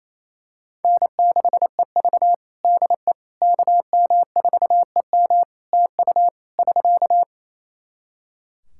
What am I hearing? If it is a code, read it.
Morse code, heard as a keyed tone, about 28 wpm, 700 Hz. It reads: N6EV DE KM4EM TU <SK>
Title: Practice QSO - Exchange 4 - Ending - 28wpm